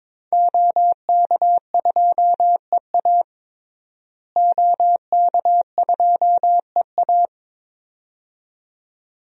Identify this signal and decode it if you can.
Morse code, heard as a keyed tone, about 22 wpm, 700 Hz. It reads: OK2EA OK2EA
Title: DXCC-100 Call Signs - 5 Characters - Repeated 2x - 22wpm